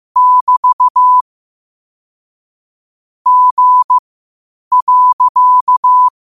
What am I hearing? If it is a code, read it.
Morse code, heard as a keyed tone, about 15 wpm, spaced as for 8 wpm, 1000 Hz. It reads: <BT> G.
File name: qotdfiles / 8wpm